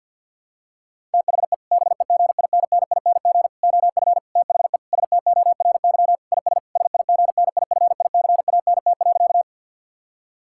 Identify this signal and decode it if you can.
Morse code, heard as a keyed tone, about 50 wpm, 700 Hz. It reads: THE BEGINNING OF THE STORY IS SIGNIFICANT.
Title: Sentences from Top 1000 Words - Review Part 6 - 50wpm